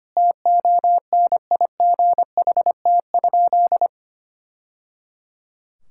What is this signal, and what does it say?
Morse code, heard as a keyed tone, about 25 wpm, 700 Hz. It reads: TONIGHT?
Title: Sentences from Top 1000 Words - Review Part 5 - 25wpm